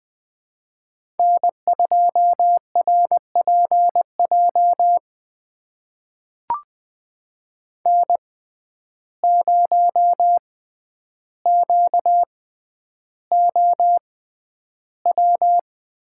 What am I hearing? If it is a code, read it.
Morse code, heard as a keyed tone, about 20 wpm, spaced as for 10 wpm, 700 Hz. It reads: N2RPJ N0QOW
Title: USA Call Signs - 5 Characters - 6x Character Spacing - 20wpm